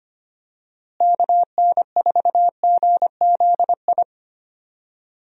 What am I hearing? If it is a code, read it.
Morse code, heard as a keyed tone, about 25 wpm, 700 Hz. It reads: KN4GZI
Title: USA Call Signs - 6 Characters - Repeated 3x - 25wpm